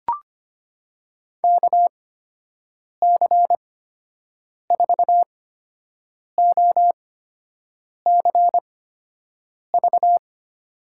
Morse code, heard as a keyed tone, about 25 wpm, spaced as for 7 wpm, 700 Hz. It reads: KC4OCV
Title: USA Call Signs - 6 Characters - 8x Character Spacing - 25wpm